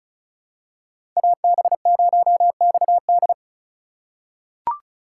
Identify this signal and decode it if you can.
Morse code, heard as a keyed tone, about 34 wpm, 700 Hz. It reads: AB0XD
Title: USA Call Signs - 5 Characters - 4x Character Spacing - 35wpm